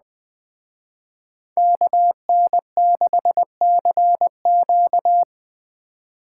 Morse code, heard as a keyed tone, about 20 wpm, 700 Hz. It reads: KN6CQ
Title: Speed Racing - USA 2x2 Call Signs 17wpm